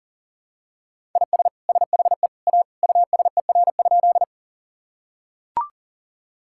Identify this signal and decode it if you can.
Morse code, heard as a keyed tone, about 40 wpm, 700 Hz. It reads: IS SHE A USER?
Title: Sentences from Top 1000 Words - Review Part 5 - 40wpm